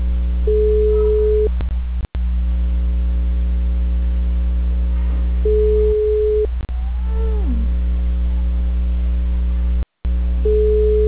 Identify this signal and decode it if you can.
Ringback tone (Europe, 425 Hz).